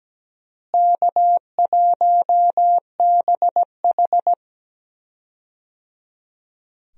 Morse code, heard as a keyed tone, about 17 wpm, 700 Hz. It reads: K1BH
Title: USA Call Signs - 4 Characters - Repeated 3x - 17wpm